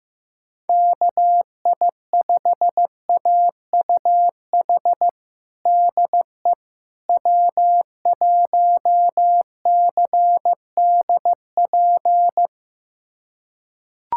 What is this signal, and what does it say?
Morse code, heard as a keyed tone, about 15 wpm, 700 Hz. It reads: KI5AUH DE W1CDP
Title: USA Callsign DE Callsign 15wpm